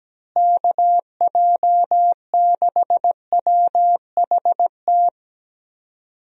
Morse code, heard as a keyed tone, about 17 wpm, 700 Hz. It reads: KJ6WHT
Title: USA Call Signs - 6 Characters - Repeated 3x - 17wpm